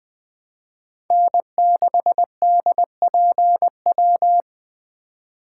Morse code, heard as a keyed tone, about 20 wpm, 700 Hz. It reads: N6DPW